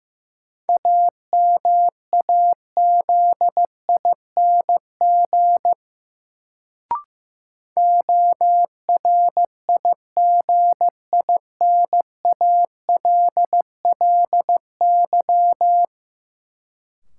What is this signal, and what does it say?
Morse code, heard as a keyed tone, about 15 wpm, 700 Hz. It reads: AMAZING ORIGINALLY
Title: Sentences from Top 1500 Words - New Word Drill Alt - 15wpm